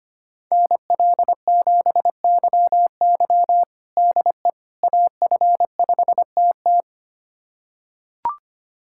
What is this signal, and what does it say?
Morse code, heard as a keyed tone, about 25 wpm, 700 Hz. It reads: NL7YY DE AF5TT
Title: USA Callsign DE Callsign - 5 Characters - 25wpm